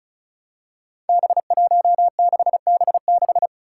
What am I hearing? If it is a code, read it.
Morse code, heard as a keyed tone, about 36 wpm, 700 Hz. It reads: B16B6
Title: Five Challenging Letter-Number Combinations 35wpm